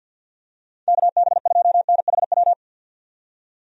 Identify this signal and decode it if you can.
Morse code, heard as a keyed tone, about 50 wpm, 700 Hz. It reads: KB2NHW
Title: USA Call Signs - 6 Characters - 50wpm